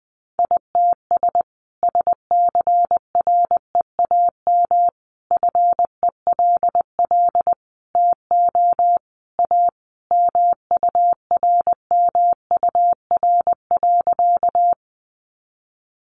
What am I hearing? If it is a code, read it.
Morse code, heard as a keyed tone, about 20 wpm, 700 Hz. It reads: ITS SCREAM FELL TO A MURMUR.
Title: A Sound Of Thunder 20wpm